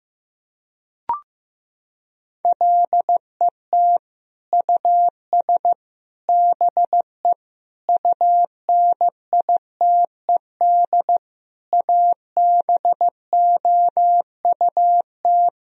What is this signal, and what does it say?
Morse code, heard as a keyed tone, about 15 wpm, 700 Hz. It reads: LET US BE UNITED ABOUT
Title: Sentences from Top 1000 Words - Review Part 4 - 15wpm